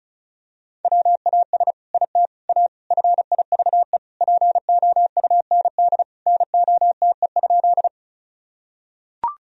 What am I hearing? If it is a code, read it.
Morse code, heard as a keyed tone, about 34 wpm, 700 Hz. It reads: WAS IT A FIVE POUND NOTE?
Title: Sentences from Top 1000 Words - Review Part 2 - 35wpm